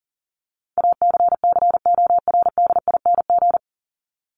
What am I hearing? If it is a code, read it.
Morse code, heard as a keyed tone, about 40 wpm, 700 Hz. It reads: ACCORDING
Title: LY Suffix - Top 100 Words with Base Words 40wpm